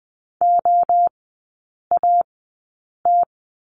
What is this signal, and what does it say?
Morse code, heard as a keyed tone, about 20 wpm, spaced as for 8 wpm, 700 Hz. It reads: OAT
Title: 060 Lesson 11e - Letter h - Three Characters 20wpm